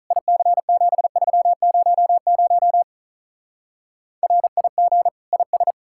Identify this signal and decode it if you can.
Morse code, heard as a keyed tone, about 46 wpm, 700 Hz. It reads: IC7300 RIG IS